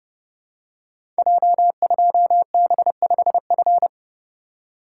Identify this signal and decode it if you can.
Morse code, heard as a keyed tone, about 30 wpm, 700 Hz. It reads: J2B5F
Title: Code Groups - 5 Letter:Numbers - 30wpm